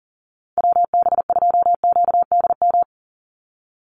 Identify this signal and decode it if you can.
Morse code, heard as a keyed tone, about 40 wpm, 690 Hz. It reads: WB2QDM
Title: USA Call Signs 40wpm